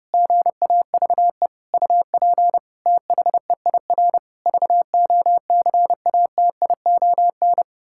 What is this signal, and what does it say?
Morse code, heard as a keyed tone, about 30 wpm, 700 Hz. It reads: GAVE UP THEIR VOCATION
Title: Pangram Challenge v2 - 30wpm